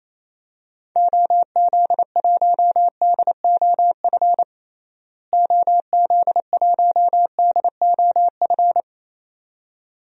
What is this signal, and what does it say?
Morse code, heard as a keyed tone, about 28 wpm, 700 Hz. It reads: OZ1DOF OZ1DOF
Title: DXCC-100 Call Signs - 6 Characters - Repeated 2x - 28wpm